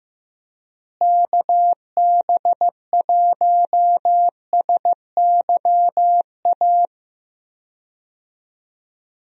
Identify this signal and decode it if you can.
Morse code, heard as a keyed tone, about 15 wpm, 700 Hz. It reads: KB1SYA
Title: USA Call Signs - 6 Characters - Repeated 2x - 15wpm